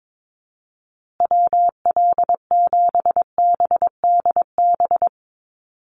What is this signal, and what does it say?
Morse code, heard as a keyed tone, about 22 wpm, 700 Hz. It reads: WL7BDB
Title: USA Call Signs Repeated 2x 22wpm